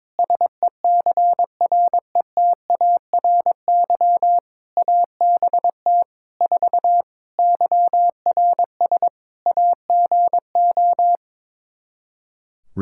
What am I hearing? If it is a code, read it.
Morse code, heard as a keyed tone, about 22 wpm, 700 Hz. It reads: SECRETARY ABT 4 YRS AGO
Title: Practice QSO - Exchange 3 - More Info - 22wpm